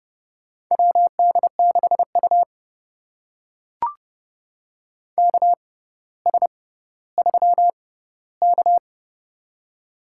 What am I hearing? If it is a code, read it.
Morse code, heard as a keyed tone, about 32 wpm, spaced as for 16 wpm, 700 Hz. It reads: WD6U KS3K
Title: USA Call Signs - 4 Characters - 6x Character Spacing - 30wpm